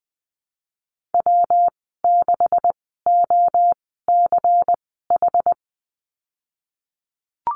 Morse code, heard as a keyed tone, about 20 wpm, spaced as for 15 wpm, 700 Hz. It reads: W6OCH
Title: 186 Lesson 30g - Letter b - Call Signs 20wpm